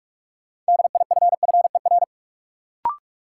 Morse code, heard as a keyed tone, about 46 wpm, 700 Hz. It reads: DIFFER
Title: Sentences from Top 1000 Words - Review Part 1 - New Word Drill - 45wpm